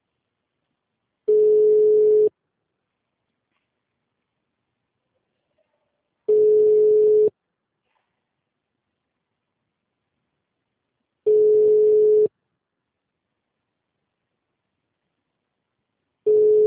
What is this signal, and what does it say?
Ringback tone (Europe, 425 Hz).
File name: out- -2001- - - .131